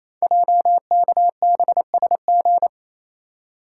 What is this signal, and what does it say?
Morse code, heard as a keyed tone, about 29 wpm, 700 Hz. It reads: JKBSG
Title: Code Groups - 5 Letters - 28wpm